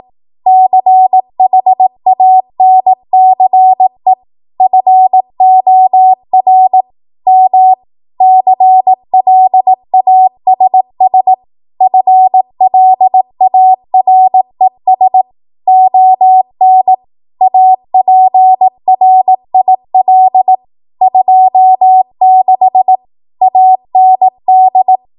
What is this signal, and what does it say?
Morse code, heard as a keyed tone, about 18 wpm, 750 Hz. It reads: CHANCE FOR M CLASS FLARES ON APRIL 26 AND
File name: Morse / 18WPM